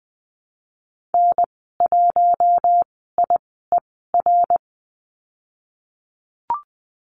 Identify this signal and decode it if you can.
Morse code, heard as a keyed tone, about 20 wpm, 700 Hz. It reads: N 1 I E R
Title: 082 Lesson 14g - Number 2 - Call Signs 20wpm